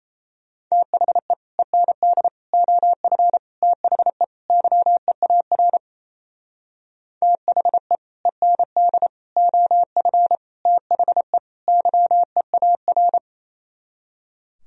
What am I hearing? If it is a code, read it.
Morse code, heard as a keyed tone, about 36 wpm, 700 Hz. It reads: THE END OF THE YEAR THE END OF THE YEAR